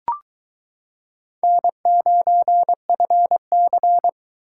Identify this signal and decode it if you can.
Morse code, heard as a keyed tone, about 22 wpm, 700 Hz. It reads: N9FC